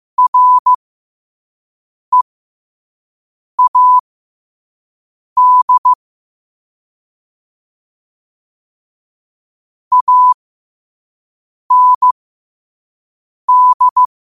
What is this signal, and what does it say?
Morse code, heard as a keyed tone, about 15 wpm, spaced as for 5 wpm, 1000 Hz. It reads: READ AND
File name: qotdfiles / 5wpm